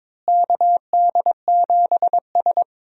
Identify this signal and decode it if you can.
Morse code, heard as a keyed tone, about 22 wpm, 700 Hz. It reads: KD7S
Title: USA Call Signs - 4 Characters - Repeated 3x - 22wpm